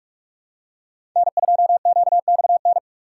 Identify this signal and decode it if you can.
Morse code, heard as a keyed tone, about 44 wpm, 700 Hz. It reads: N2QXN
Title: USA Call Signs - 5 Characters - 2x Character Spacing - 45wpm